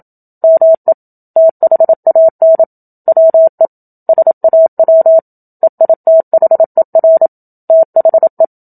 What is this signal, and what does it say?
Morse code, heard as a keyed tone, about 27 wpm, 650 Hz. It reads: ME THAN WE SAW EITHER THE
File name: Book Two / 01-Chapter 1 - 27 WPM